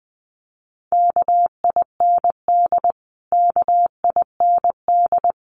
Morse code, heard as a keyed tone, about 20 wpm, 700 Hz. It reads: KIND KIND
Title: Top 200 Words Repeated 20wpm